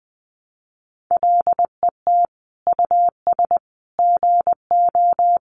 Morse code, heard as a keyed tone, about 20 wpm, 700 Hz. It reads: LET US GO
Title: Sentences from Top 300 Words 20wpm